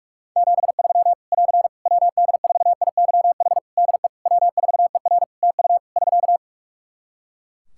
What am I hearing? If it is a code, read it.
Morse code, heard as a keyed tone, about 46 wpm, 700 Hz. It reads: 73 <AR> WD4IYH DE W4ER TU <SK>
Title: Practice QSO - Exchange 4 - Ending - 45wpm